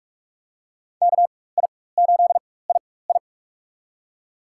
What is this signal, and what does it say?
Morse code, heard as a keyed tone, about 46 wpm, spaced as for 21 wpm, 700 Hz. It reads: KI8II
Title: USA Call Signs - 4x Character Spacing - 45wpm